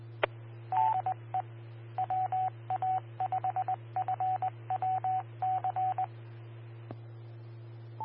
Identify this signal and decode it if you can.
Morse code, heard as a keyed tone, about 20 wpm, 750 Hz. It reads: DE WA5FWC
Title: Austin, TX Amateur Repeaters